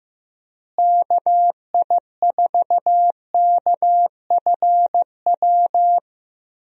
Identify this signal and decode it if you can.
Morse code, heard as a keyed tone, about 15 wpm, 700 Hz. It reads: KI4KFW
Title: USA Call Signs - 6 Characters - Repeated 3x - 15wpm